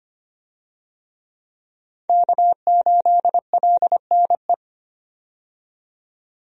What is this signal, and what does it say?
Morse code, heard as a keyed tone, about 25 wpm, 700 Hz. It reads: K8LNE
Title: USA Call Signs - 5 Characters - 25wpm